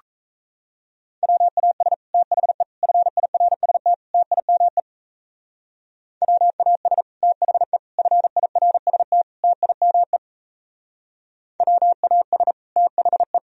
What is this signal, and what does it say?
Morse code, heard as a keyed tone, about 33 wpm, 700 Hz. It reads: WAS THE FIRST TIME WAS THE FIRST TIME WAS THE
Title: Speed-Racing - Sets of 4 Words 28wpm